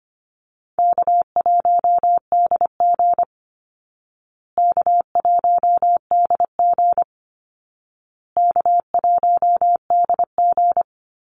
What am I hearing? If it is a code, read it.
Morse code, heard as a keyed tone, about 25 wpm, 700 Hz. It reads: K1DG K1DG K1DG
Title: Easy Call Signs Repeated 3x - 25wpm